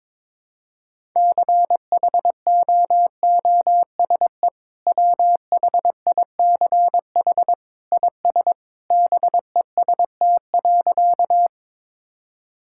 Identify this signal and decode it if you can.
Morse code, heard as a keyed tone, about 21 wpm, 700 Hz. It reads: CHOOSE WHICH IS BEST.
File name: practice / 22wpm / Sentences with 2-Word Phrases - 22wpm